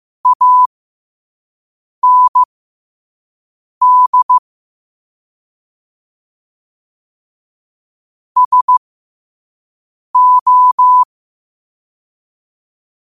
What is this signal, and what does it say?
Morse code, heard as a keyed tone, about 15 wpm, spaced as for 5 wpm, 1000 Hz. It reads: AND SO
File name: qotdfiles / 5wpm